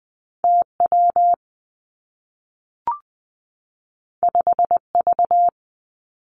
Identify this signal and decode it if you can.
Morse code, heard as a keyed tone, about 19 wpm, 700 Hz. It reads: T W 5 V
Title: Two Challenging Letter-Number Combinations 20wpm